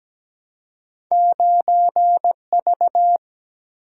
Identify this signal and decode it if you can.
Morse code, heard as a keyed tone, about 17 wpm, 700 Hz. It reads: 9V
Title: Single-Letter Single-Number 17wpm